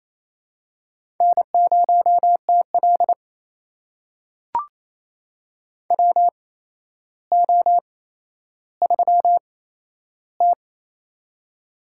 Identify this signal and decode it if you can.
Morse code, heard as a keyed tone, about 28 wpm, spaced as for 12 wpm, 700 Hz. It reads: N0TL WO3T
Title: USA Call Signs - 4 Characters - 8x Character Spacing - 28wpm